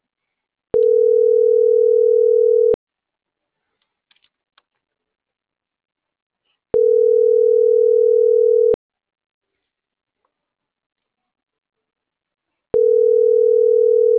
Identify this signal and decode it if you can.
Ringback tone (North America), 440 + 480 Hz.